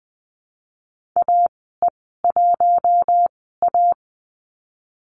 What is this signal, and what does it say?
Morse code, heard as a keyed tone, about 20 wpm, spaced as for 15 wpm, 700 Hz. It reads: AE1A
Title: 048 Lesson 9g - Number 4 - Call Signs 20wpm